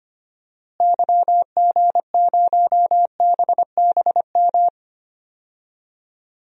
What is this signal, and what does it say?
Morse code, heard as a keyed tone, about 25 wpm, 700 Hz. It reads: YG0BBM
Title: DXCC-100 Call Signs - 6 Characters - Repeated 3x - 25wpm